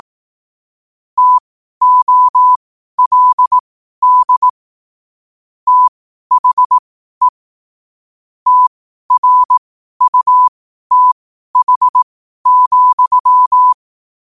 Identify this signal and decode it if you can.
Morse code, heard as a keyed tone, about 18 wpm, spaced as for 12 wpm, 1000 Hz. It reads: TOLD THE TRUTH,
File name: qotdfiles / 12wpm